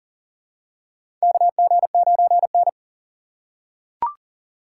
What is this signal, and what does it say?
Morse code, heard as a keyed tone, about 40 wpm, 700 Hz. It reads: KG9N
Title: USA Call Signs - 3x Character Spacing - 40wpm